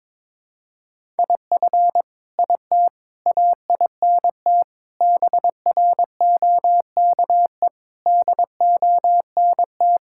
Morse code, heard as a keyed tone, about 23 wpm, 700 Hz. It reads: IF IT AINT BROKE DONT
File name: practice / 22wpm / Sets of 7 Words - 22wpm